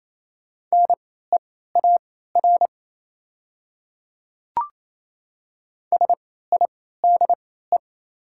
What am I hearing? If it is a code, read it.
Morse code, heard as a keyed tone, about 28 wpm, spaced as for 16 wpm, 700 Hz. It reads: NEAR SIDE
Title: 3x Character Spacing - Top 200 Words - 28wpm